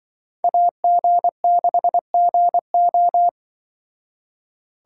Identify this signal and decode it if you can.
Morse code, heard as a keyed tone, about 25 wpm, 700 Hz. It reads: AG6GO
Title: Speed Racing - USA Call Signs 20wpm